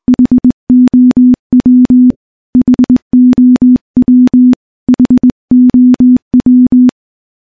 Morse code, heard as a keyed tone, about 19 wpm, 250 Hz. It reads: HOW HOW HOW
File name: morse / morse-training-20WPM